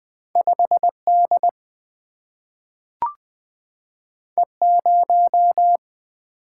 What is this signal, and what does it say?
Morse code, heard as a keyed tone, about 20 wpm, 700 Hz. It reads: 5 D E 0
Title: Single-Letter Single-Number 20wpm